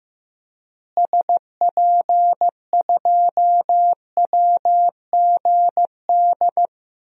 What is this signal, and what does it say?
Morse code, heard as a keyed tone, about 15 wpm, 700 Hz. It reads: SP2WGD
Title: DXCC-100 Call Signs - 6 Characters - Repeated 2x - 15wpm